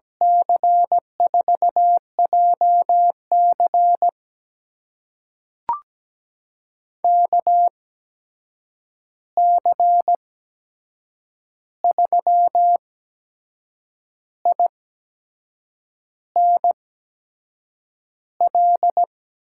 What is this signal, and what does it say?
Morse code, heard as a keyed tone, about 17 wpm, spaced as for 6 wpm, 700 Hz. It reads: C4JC KC3INL
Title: USA Call Signs - 8x Character Spacing - 17wpm